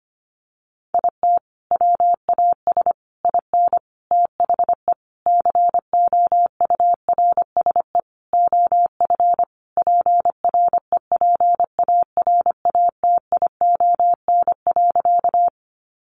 Morse code, heard as a keyed tone, about 25 wpm, 700 Hz. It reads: IT WAS IN THE COURSE OF PREPARATION.
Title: Sentences from Top 1400 Words 25wpm